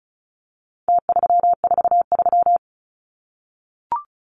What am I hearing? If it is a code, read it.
Morse code, heard as a keyed tone, about 34 wpm, 700 Hz. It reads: T343
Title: Numbers - Contest Style - 4-Digit 35wpm